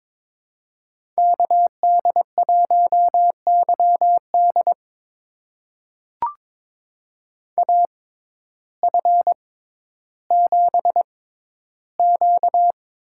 Morse code, heard as a keyed tone, about 22 wpm, spaced as for 12 wpm, 700 Hz. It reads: KD1YD AF7Q
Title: USA Call Signs - 5 Characters - 6x Character Spacing - 22wpm